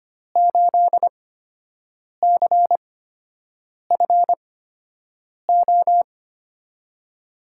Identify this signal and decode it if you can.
Morse code, heard as a keyed tone, about 25 wpm, spaced as for 7 wpm, 700 Hz. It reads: 8CFO
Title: USA Call Signs - 5 Characters - 8x Character Spacing - 25wpm